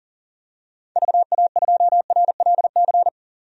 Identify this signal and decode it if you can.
Morse code, heard as a keyed tone, about 37 wpm, 700 Hz. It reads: VA2RLC
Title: DXCC-100 Call Signs - 6 Characters - 40wpm